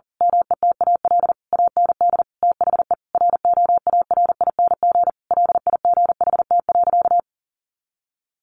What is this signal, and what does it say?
Morse code, heard as a keyed tone, about 40 wpm, 700 Hz. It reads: METAL AND THE ROARING LIGHT.
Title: A Sound Of Thunder 40wpm